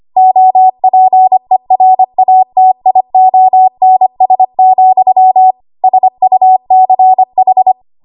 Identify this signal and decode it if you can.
Morse code, heard as a keyed tone, about 25 wpm, 750 Hz. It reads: OPERATIONS, SUCH
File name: Morse / 25WPM